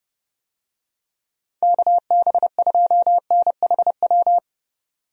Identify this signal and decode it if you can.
Morse code, heard as a keyed tone, about 30 wpm, 700 Hz. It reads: KB2NHW
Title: USA Call Signs - 6 Characters - 30wpm